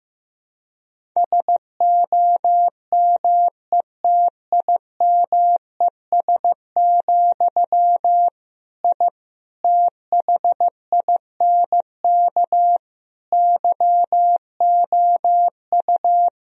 Morse code, heard as a keyed tone, about 15 wpm, 700 Hz. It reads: SOMETIMES, I THINK YOU
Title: Sentences from Top 1000 Words - Review Part 3 - 15wpm